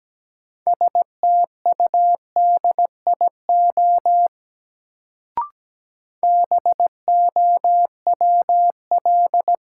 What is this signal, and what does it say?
Morse code, heard as a keyed tone, about 17 wpm, 700 Hz. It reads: STUDIO BOWL
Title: Sentences from Top 1200 Words - New Word Drill Alt - 17wpm